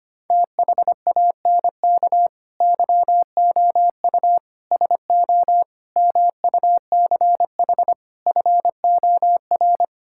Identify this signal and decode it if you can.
Morse code, heard as a keyed tone, about 25 wpm, 700 Hz. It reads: THANK YOU SO MUCH FOR
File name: practice / 25wpm / Sentences with 9-Word Phrases - 25wpm